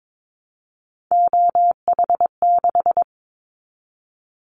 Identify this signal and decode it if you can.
Morse code, heard as a keyed tone, about 22 wpm, 700 Hz. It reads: OH6
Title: Callsign Prefixes for Top 25 DXCC - Repeated 3x - 22wpm